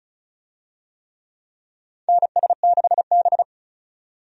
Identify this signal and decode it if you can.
Morse code, heard as a keyed tone, about 34 wpm, 700 Hz. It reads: NS6B
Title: USA Call Signs - 4 Characters - 35wpm